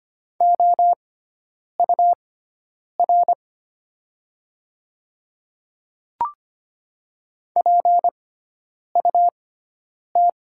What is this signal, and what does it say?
Morse code, heard as a keyed tone, about 25 wpm, spaced as for 9 wpm, 700 Hz. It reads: OUR PUT
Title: 6x Character Spacing - Top 500 Words - 25wpm